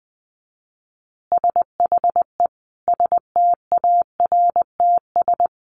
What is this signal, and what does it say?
Morse code, heard as a keyed tone, about 20 wpm, 700 Hz. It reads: SHE STARTS
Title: Easy Phrases 20wpm